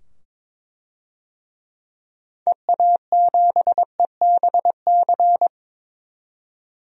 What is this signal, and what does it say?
Morse code, heard as a keyed tone, about 21 wpm, 700 Hz. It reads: EA7EBC
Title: DXCC-100 Call Signs - 6 Characters - 22wpm